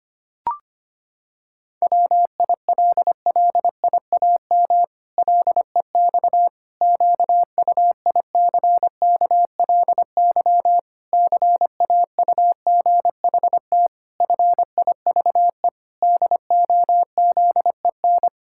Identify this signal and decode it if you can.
Morse code, heard as a keyed tone, about 25 wpm, 700 Hz. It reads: WILLIAM LEX QUICKLY CAUGHT FIVE DOZEN
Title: Pangram Challenge v2 - 25wpm